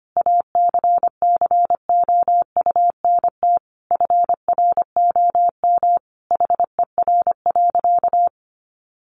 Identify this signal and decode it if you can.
Morse code, heard as a keyed tone, about 25 wpm, 700 Hz. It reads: ACCOUNT FROM HER.
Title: Sentences from Top 900 Words 25wpm